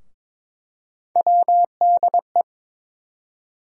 Morse code, heard as a keyed tone, about 22 wpm, 700 Hz. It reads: WDE
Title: USA Call Sign Postfixes - 3 Letters 22wpm